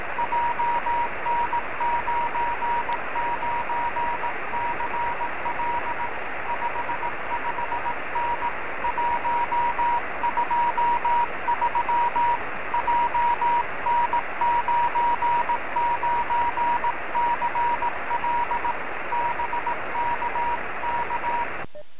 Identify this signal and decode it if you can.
Morse code, heard as a keyed tone, about 18 wpm, 950 Hz, with radio noise. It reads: JN99CL 55N 123 JN99CL BKK
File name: sm7lcb / 09-ol9w-23cm